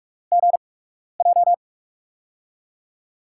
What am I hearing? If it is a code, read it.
Morse code, heard as a keyed tone, about 44 wpm, spaced as for 12 wpm, 690 Hz. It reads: GJ
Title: USA Call Signs - 4 Characters - 8x Character Spacing - 45wpm